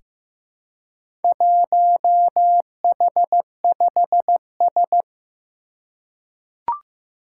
Morse code, heard as a keyed tone, about 15 wpm, 700 Hz. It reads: 1H5S
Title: Four Challenging Letter-Number Combinations 15wpm